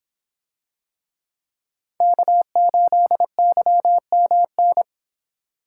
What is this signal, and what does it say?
Morse code, heard as a keyed tone, about 25 wpm, 700 Hz. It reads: K8YMN